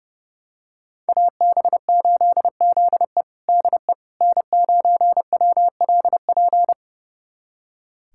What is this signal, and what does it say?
Morse code, heard as a keyed tone, about 30 wpm, 700 Hz. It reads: AB8ZE DE N9WLP
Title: USA Callsign DE Callsign - 5 Characters - 30wpm